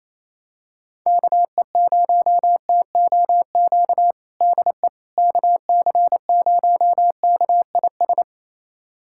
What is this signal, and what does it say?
Morse code, heard as a keyed tone, about 28 wpm, 700 Hz. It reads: KE0TOQ DE KC0KIS
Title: USA Callsign DE Callsign - 6 Characters - 28wpm